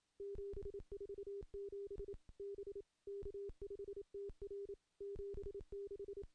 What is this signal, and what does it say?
Morse code, heard as a keyed tone, about 28 wpm, 400 Hz. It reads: 747 B K5TR 76
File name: contests / CW-SS-k5tr- -0600